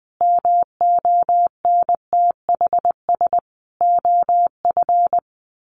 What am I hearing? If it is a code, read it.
Morse code, heard as a keyed tone, about 19 wpm, 700 Hz. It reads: MONTHS OF
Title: Sets of 5 Words Encore2 20wpm